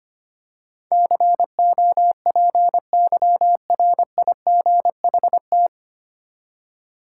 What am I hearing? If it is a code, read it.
Morse code, heard as a keyed tone, about 25 wpm, 700 Hz. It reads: COPYRIGHT
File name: Nine Letter Words - 25wpm